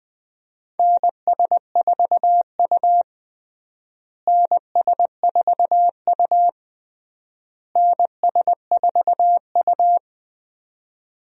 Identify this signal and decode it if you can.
Morse code, heard as a keyed tone, about 21 wpm, 700 Hz. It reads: NS4U NS4U NS4U
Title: USA Call Signs - 4 Characters - Repeated 3x - 20wpm